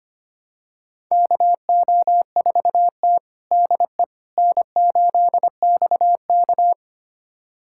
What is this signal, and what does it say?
Morse code, heard as a keyed tone, about 25 wpm, 700 Hz. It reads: KO4T DE N8XK
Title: USA Callsign DE Callsign - 4 Characters - 25wpm